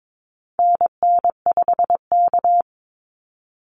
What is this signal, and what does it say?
Morse code, heard as a keyed tone, about 22 wpm, 700 Hz. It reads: NN5K
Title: NA SOTA Activator Callsigns 22wpm